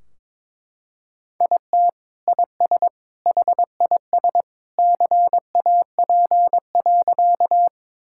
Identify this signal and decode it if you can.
Morse code, heard as a keyed tone, about 22 wpm, 700 Hz. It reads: IT IS HIS CAP.
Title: Sentences from Top 1600 Words 22wpm